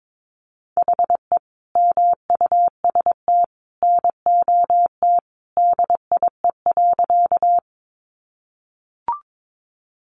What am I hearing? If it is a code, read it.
Morse code, heard as a keyed tone, about 22 wpm, 700 Hz. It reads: HE MUST NOT DIE.
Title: Sentences from Top 1300 Words 22wpm